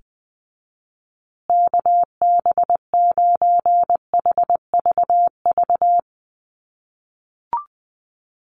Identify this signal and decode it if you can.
Morse code, heard as a keyed tone, about 20 wpm, 700 Hz. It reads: KB9HVV
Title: USA Call Signs Repeated 3x 20wpm